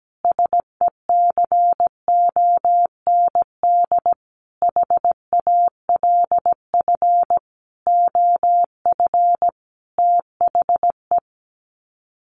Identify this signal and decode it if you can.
Morse code, heard as a keyed tone, about 17 wpm, 700 Hz. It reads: SECOND HALF OF THE
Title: Sets of 5 Words 17wpm